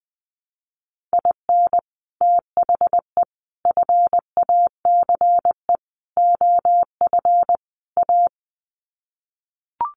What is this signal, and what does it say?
Morse code, heard as a keyed tone, about 20 wpm, 700 Hz. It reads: IN THE FACE OF A
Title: Sets of 5 Words Encore 20wpm